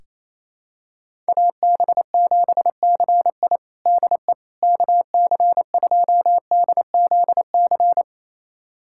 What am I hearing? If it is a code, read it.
Morse code, heard as a keyed tone, about 28 wpm, 700 Hz. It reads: AB7CI DE KC2DZC
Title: USA Callsign DE Callsign 28wpm